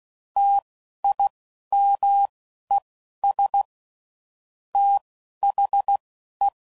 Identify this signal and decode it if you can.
Morse code, heard as a keyed tone, about 16 wpm, spaced as for 11 wpm, 800 Hz. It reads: TIMES THE
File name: CW Code Groups @10WPM / Technical Subjects / XL